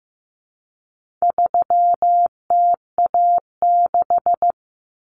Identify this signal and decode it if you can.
Morse code, heard as a keyed tone, about 15 wpm, 700 Hz. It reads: 3TA6
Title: Common Cut Numbers - 4-Digit 15wpm